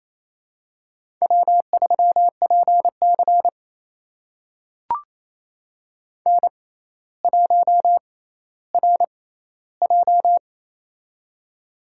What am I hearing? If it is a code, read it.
Morse code, heard as a keyed tone, about 28 wpm, spaced as for 14 wpm, 700 Hz. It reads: W3PC N1RJ
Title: USA Call Signs - 4 Characters - 6x Character Spacing - 28wpm